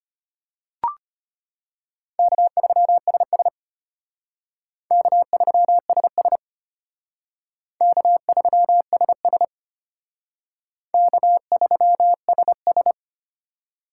Morse code, heard as a keyed tone, about 33 wpm, 700 Hz. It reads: K3SS K3SS K3SS K3SS